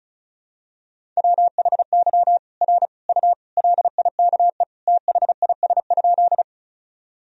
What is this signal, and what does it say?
Morse code, heard as a keyed tone, about 36 wpm, 700 Hz. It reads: WHY R U LIKE THIS?
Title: Sentences from Top 200 Words and Abbreviations 35wpm